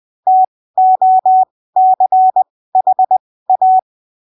Morse code, heard as a keyed tone, about 20 wpm, spaced as for 15 wpm, 750 Hz. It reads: TOCHA
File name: arquivos / texto / texto 08 20ppm